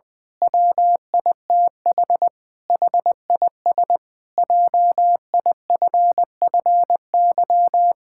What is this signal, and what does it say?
Morse code, heard as a keyed tone, about 19 wpm, 700 Hz. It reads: WITH HIS JIFFY
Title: Pangram Challenge v2 - 20wpm